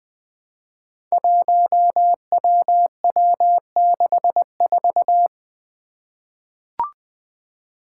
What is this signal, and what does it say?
Morse code, heard as a keyed tone, about 20 wpm, 700 Hz. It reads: 1WW64
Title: Five Challenging Letter-Number Combinations 20wpm